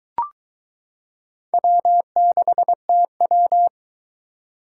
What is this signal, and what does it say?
Morse code, heard as a keyed tone, about 22 wpm, 700 Hz. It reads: W6TW